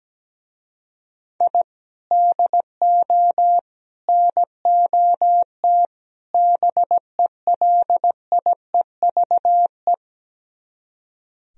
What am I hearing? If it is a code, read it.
Morse code, heard as a keyed tone, about 17 wpm, 700 Hz. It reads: I DO NOT BELIEVE
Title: Speed-Racing - Sets of 4 Words 17wpm